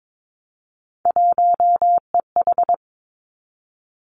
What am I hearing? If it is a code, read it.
Morse code, heard as a keyed tone, about 22 wpm, 700 Hz. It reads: 1EH
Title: Three Challenging Letter-Number Combinations 22wpm